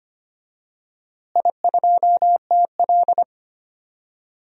Morse code, heard as a keyed tone, about 25 wpm, 700 Hz. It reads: I2TL
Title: DXCC-100 Call Signs - 4 Characters - Repeated 3x - 25wpm